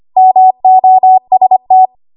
Morse code, heard as a keyed tone, about 25 wpm, 750 Hz. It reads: MOST
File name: Morse / 25WPM